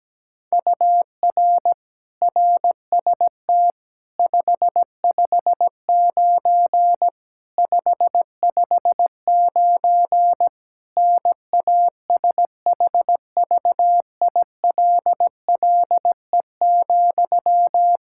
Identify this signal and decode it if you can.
Morse code, heard as a keyed tone, about 17 wpm, 700 Hz. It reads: UR RST 559 559 NASHVILLE,
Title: Practice QSO - Exchange 1 - Beginning - 17wpm